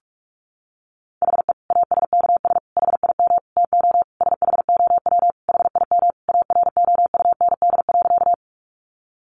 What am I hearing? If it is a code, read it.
Morse code, heard as a keyed tone, about 44 wpm, 700 Hz. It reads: HE ASKS HIM TO SHOW HIM AROUND.
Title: Easy Phrases 45wpm